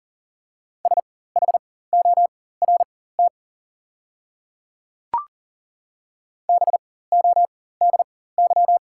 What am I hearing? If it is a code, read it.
Morse code, heard as a keyed tone, about 37 wpm, spaced as for 18 wpm, 700 Hz. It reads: SHORT BODY
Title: 4x Character Spacing - Top 500 Words - 40wpm